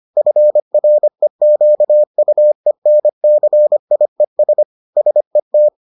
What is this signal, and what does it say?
Morse code, heard as a keyed tone, about 25 wpm, 600 Hz. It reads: FREQUENCIES SET